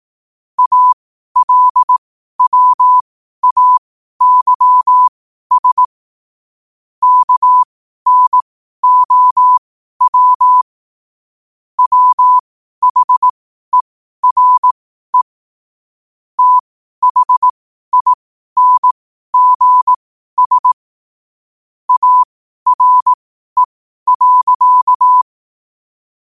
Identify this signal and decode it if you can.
Morse code, heard as a keyed tone, about 18 wpm, spaced as for 12 wpm, 1000 Hz. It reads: ALWAYS KNOW WHERE THINGS ARE.
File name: qotdfiles / 12wpm